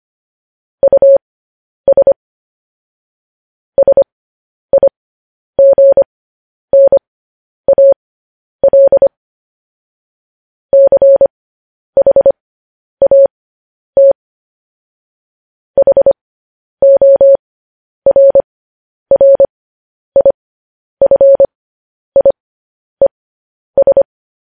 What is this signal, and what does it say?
Morse code, heard as a keyed tone, about 25 wpm, spaced as for 10 wpm, 550 Hz. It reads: US SIGNAL CHAT HORRIFIES
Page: news-headlines-in-morse-code-at-10-wpm